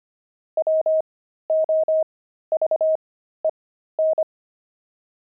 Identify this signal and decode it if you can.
Morse code, heard as a keyed tone, about 25 wpm, spaced as for 13 wpm, 630 Hz. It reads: WOVEN
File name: pangram-fs-13-cs-25-sixty